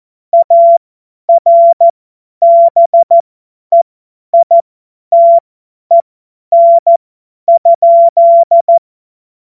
Morse code, heard as a keyed tone, about 14 wpm, spaced as for 10 wpm, 680 Hz. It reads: ARBEITEN?
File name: Morsekurs / Witze / Witz 23. Dezember 2025 50 F70